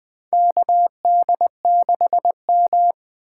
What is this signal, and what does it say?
Morse code, heard as a keyed tone, about 20 wpm, 700 Hz. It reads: KD6M
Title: USA Call Signs - 4 Characters - 4x Character Spacing - 20wpm